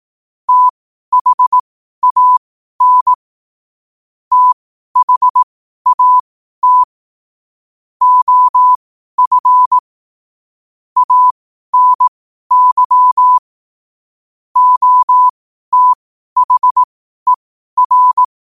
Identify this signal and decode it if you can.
Morse code, heard as a keyed tone, about 18 wpm, spaced as for 12 wpm, 1000 Hz. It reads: THAN THAT OF ANY OTHER
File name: qotdfiles / 12wpm